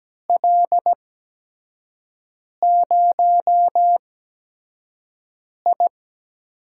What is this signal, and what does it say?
Morse code, heard as a keyed tone, about 17 wpm, spaced as for 5 wpm, 700 Hz. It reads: L0I
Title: USA Call Signs - 4 Characters - 8x Character Spacing - 17wpm